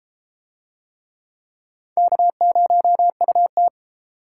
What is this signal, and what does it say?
Morse code, heard as a keyed tone, about 32 wpm, 700 Hz. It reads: K0UT